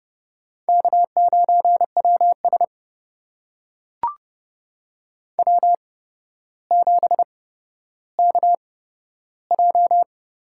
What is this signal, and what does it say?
Morse code, heard as a keyed tone, about 30 wpm, spaced as for 13 wpm, 700 Hz. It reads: K9WS W7KJ
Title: USA Call Signs - 4 Characters - 8x Character Spacing - 30wpm